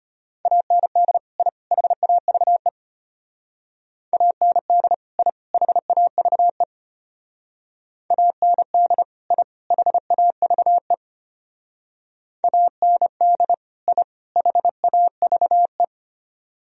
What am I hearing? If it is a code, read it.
Morse code, heard as a keyed tone, about 31 wpm, 700 Hz. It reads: AND I HAVE AND I HAVE AND I HAVE AND I HAVE
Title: Speed-Racing - Sets of 3 Words 25wpm